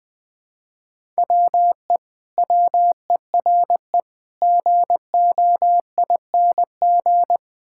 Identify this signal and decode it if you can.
Morse code, heard as a keyed tone, about 20 wpm, 700 Hz. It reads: WE WERE GOING
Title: Build A Phrase - 4-Word Phrases - 20wpm